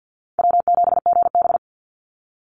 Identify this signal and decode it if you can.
Morse code, heard as a keyed tone, about 50 wpm, 700 Hz. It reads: W7GB
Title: Easy Call Signs - 50wpm